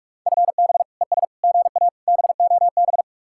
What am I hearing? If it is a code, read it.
Morse code, heard as a keyed tone, about 50 wpm, 700 Hz. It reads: FB ES GA BOB